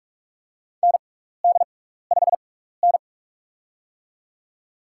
Morse code, heard as a keyed tone, about 46 wpm, spaced as for 16 wpm, 700 Hz. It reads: ND5N
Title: USA Call Signs - 4 Characters - 6x Character Spacing - 45wpm